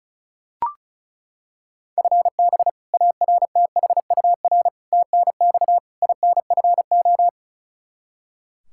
Morse code, heard as a keyed tone, about 34 wpm, 700 Hz. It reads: FB ARTHUR TNX INFO
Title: Practice QSO - Exchange 3 - More Info - 35wpm